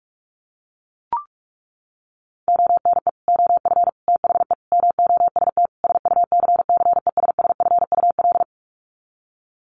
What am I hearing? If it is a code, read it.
Morse code, heard as a keyed tone, about 44 wpm, 700 Hz. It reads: ONE OF THE MOST SUCCESSFUL
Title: Sets of 5 Words Encore2 45wpm